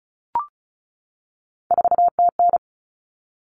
Morse code, heard as a keyed tone, about 34 wpm, 700 Hz. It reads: 4TN
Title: Common Cut Numbers - 3-Digit 35wpm